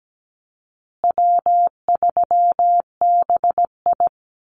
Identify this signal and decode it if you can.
Morse code, heard as a keyed tone, about 17 wpm, 700 Hz. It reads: W3BI
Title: Easy Call Signs Repeated 3x - 17wpm